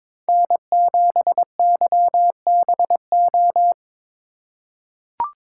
Morse code, heard as a keyed tone, about 22 wpm, 700 Hz. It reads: N7YBO
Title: USA Call Signs - 5 Characters - 4x Character Spacing - 22wpm